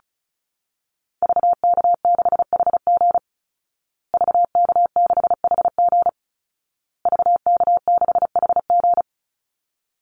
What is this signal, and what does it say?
Morse code, heard as a keyed tone, about 34 wpm, 700 Hz. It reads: VK6HG VK6HG VK6HG
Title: Call Signs Repeated 2x - 35wpm